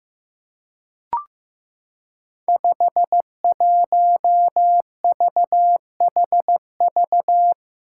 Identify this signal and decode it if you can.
Morse code, heard as a keyed tone, about 15 wpm, 700 Hz. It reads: 51VHV
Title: Five Challenging Letter-Number Combinations 15wpm